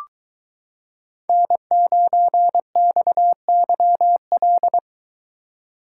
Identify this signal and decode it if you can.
Morse code, heard as a keyed tone, about 23 wpm, 700 Hz. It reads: N9XYL